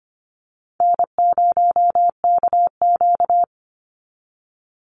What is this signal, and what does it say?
Morse code, heard as a keyed tone, about 25 wpm, 700 Hz. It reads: N0KQ
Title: Easy Call Signs Repeated 2x - 25wpm